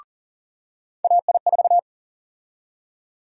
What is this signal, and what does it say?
Morse code, heard as a keyed tone, about 40 wpm, 700 Hz. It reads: AI4
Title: USA Call Sign Prefixes - 40wpm